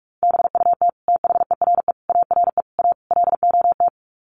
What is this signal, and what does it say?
Morse code, heard as a keyed tone, about 44 wpm, 700 Hz. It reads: BUT THERE ARE A LOT
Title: Sets of 5 Words Encore2 45wpm